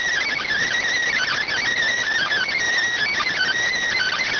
Emergency Alert System (SAME) tones, heard as harsh buzzing bursts. It is heard from 1.1 to 3.6 s.